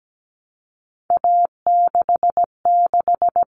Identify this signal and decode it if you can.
Morse code, heard as a keyed tone, about 17 wpm, 700 Hz. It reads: A66
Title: All Cut Numbers - 3-Digit 17wpm